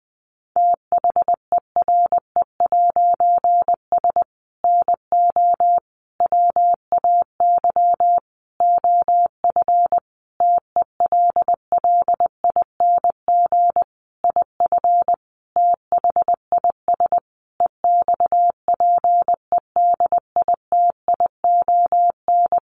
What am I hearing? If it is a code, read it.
Morse code, heard as a keyed tone, about 20 wpm, 700 Hz. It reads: THERE'S NO WAY OF TELLING IF THIS EXPEDITION
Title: A Sound Of Thunder 20wpm